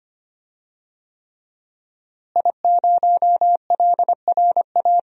Morse code, heard as a keyed tone, about 25 wpm, 700 Hz. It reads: I0LRA
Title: DXCC-100 Call Signs - 5 Characters - Repeated 3x - 25wpm